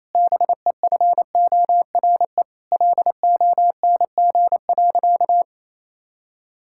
Morse code, heard as a keyed tone, about 28 wpm, 700 Hz. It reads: BEFORE LONG.
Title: Sentences from Top 1600 Words 28wpm